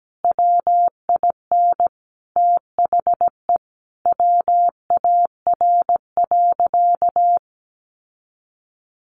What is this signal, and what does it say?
Morse code, heard as a keyed tone, about 17 wpm, 700 Hz. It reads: WIN THE WAR.
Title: Sentences from Top 1500 Words 17wpm